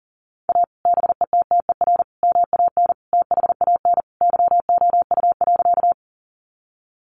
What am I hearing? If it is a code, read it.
Morse code, heard as a keyed tone, about 40 wpm, 700 Hz. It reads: A BETTER MAN THAN YOU.
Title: Sentences from Top 400 Words 40wpm